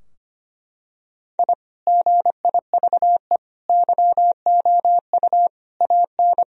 Morse code, heard as a keyed tone, about 25 wpm, 700 Hz. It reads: I GIVE YOU AN
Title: Sentences from Top 1000 Words - Review Part 2 - 25wpm